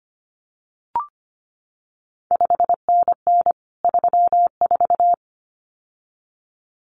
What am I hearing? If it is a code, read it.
Morse code, heard as a keyed tone, about 25 wpm, 700 Hz. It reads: 5NN 34
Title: CQ WW DX Contest Exchange Encore - 25wpm